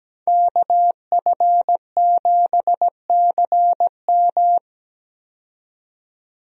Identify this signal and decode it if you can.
Morse code, heard as a keyed tone, about 17 wpm, 700 Hz. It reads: KF7CM
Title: USA Call Signs - 5 Characters - 17wpm